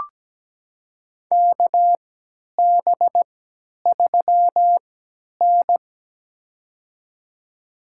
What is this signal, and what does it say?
Morse code, heard as a keyed tone, about 17 wpm, spaced as for 10 wpm, 700 Hz. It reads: KB3N
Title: USA Call Signs - 4 Characters - 3x Character Spacing - 17wpm